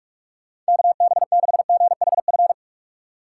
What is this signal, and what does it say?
Morse code, heard as a keyed tone, about 58 wpm, 700 Hz. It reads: KB6GHF